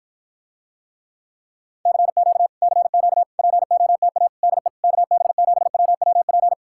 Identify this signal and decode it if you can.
Morse code, heard as a keyed tone, about 54 wpm, 700 Hz. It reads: CQ CQ POTA DE KB7WWP